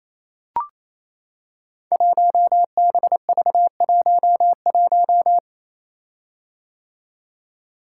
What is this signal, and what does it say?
Morse code, heard as a keyed tone, about 29 wpm, 700 Hz. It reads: 1BV11
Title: Code Groups - 5 Letter:Numbers - 28wpm